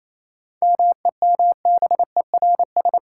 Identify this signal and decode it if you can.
Morse code, heard as a keyed tone, about 28 wpm, 700 Hz. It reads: MEMBERS
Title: Sets of 3 Words - New Word Drill - 28wpm